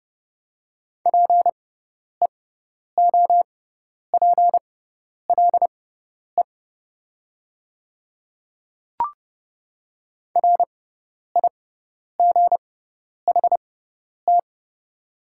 Morse code, heard as a keyed tone, about 29 wpm, spaced as for 10 wpm, 700 Hz. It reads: PEOPLE RIGHT
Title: 6x Character Spacing - Top 200 Words - 30wpm